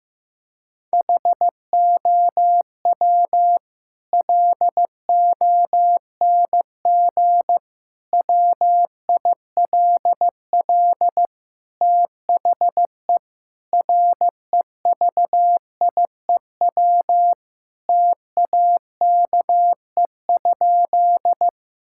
Morse code, heard as a keyed tone, about 16 wpm, 700 Hz. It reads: HOW LONG WILL THE REVIEW TAKE?
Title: Sentences from Top 1000 Words - Review Part 6 - 15wpm